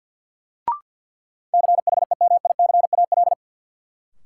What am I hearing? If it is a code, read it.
Morse code, heard as a keyed tone, about 52 wpm, 700 Hz. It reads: CHEMICAL
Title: Sentences from Top 1200 Words - New Word Drill - 50wpm